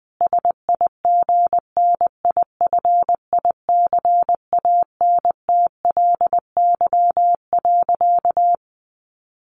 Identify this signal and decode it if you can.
Morse code, heard as a keyed tone, about 20 wpm, 700 Hz. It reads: SIGNIFICANTLY.
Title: Sentences from Top 1500 Words 20wpm